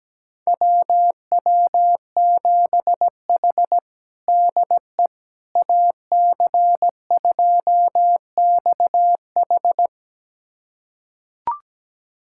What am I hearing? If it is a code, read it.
Morse code, heard as a keyed tone, about 17 wpm, 700 Hz. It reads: WW7H DE AC2XH
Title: USA Callsign DE Callsign 17wpm